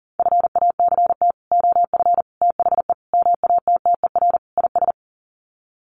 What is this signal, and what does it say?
Morse code, heard as a keyed tone, about 40 wpm, 700 Hz. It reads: FACT OF THE MATTER IS
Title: Sets of 5 Words 40wpm